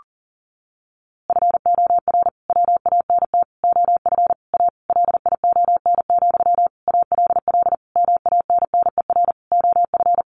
Morse code, heard as a keyed tone, about 40 wpm, 700 Hz. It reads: FOR WANT OF A LION, ALL MANNER OF
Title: A Sound Of Thunder 40wpm